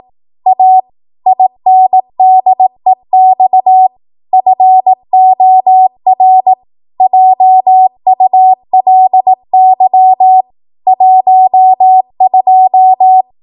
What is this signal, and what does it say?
Morse code, heard as a keyed tone, about 18 wpm, 750 Hz. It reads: A INDEX FOR JULY 12
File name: Morse / 18WPM